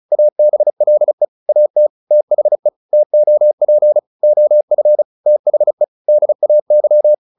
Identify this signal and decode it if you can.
Morse code, heard as a keyed tone, about 34 wpm, 600 Hz. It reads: ABLE AT THE TOP OF THE DAY
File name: WD205 35